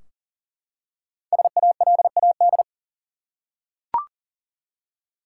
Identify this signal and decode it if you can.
Morse code, heard as a keyed tone, about 40 wpm, 700 Hz. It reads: SALAD
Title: Five Letter Words - 40wpm